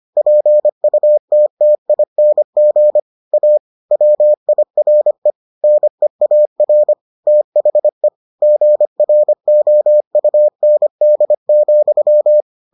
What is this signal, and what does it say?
Morse code, heard as a keyed tone, about 25 wpm, 600 Hz. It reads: PUTTING A WIRE NEAR THE GROUND,